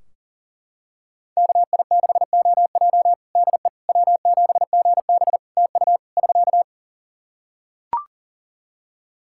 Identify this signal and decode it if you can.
Morse code, heard as a keyed tone, about 40 wpm, 700 Hz. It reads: KI6OJ DE W7GB TU <SK>
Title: Practice QSO - Exchange 4 - Ending - 40wpm